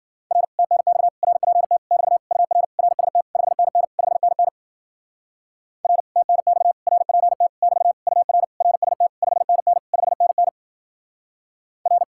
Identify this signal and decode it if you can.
Morse code, heard as a keyed tone, about 53 wpm, 700 Hz. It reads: R TNX RPT <BT> UR RST 5NN 5NN R TNX RPT <BT> UR RST 5NN 5NN R